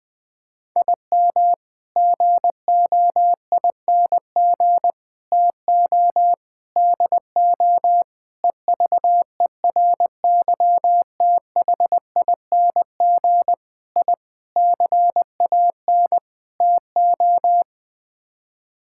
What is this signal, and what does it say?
Morse code, heard as a keyed tone, about 20 wpm, 700 Hz. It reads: IM GOING TO DO EVERYTHING I CAN TO
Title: Sets of 8 Words - 20wpm